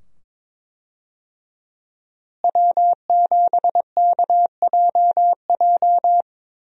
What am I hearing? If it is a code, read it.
Morse code, heard as a keyed tone, about 22 wpm, 700 Hz. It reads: W7KJJ
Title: USA Call Signs - 5 Characters - Repeated 2x - 22wpm